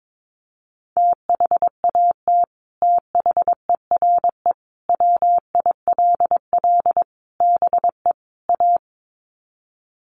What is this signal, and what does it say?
Morse code, heard as a keyed tone, about 22 wpm, 700 Hz. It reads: THAT THERE WILL BE A
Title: Sets of 5 Words Encore2 22wpm